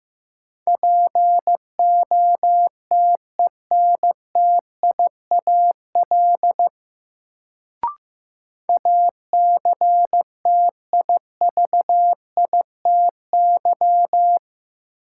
Morse code, heard as a keyed tone, about 15 wpm, 700 Hz. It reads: POTENTIAL ACTIVITY
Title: Sentences from Top 1000 Words - Review Part 4 - New Word Drill - 15wpm